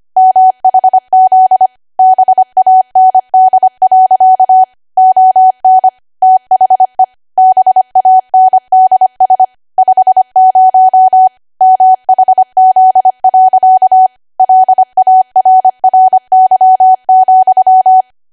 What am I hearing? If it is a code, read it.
Morse code, heard as a keyed tone, about 25 wpm, 750 Hz. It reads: MHZ BAND. ON THE BANDS 50 MHZ. LARRY,
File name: Archive / 25 WPM / 25WPM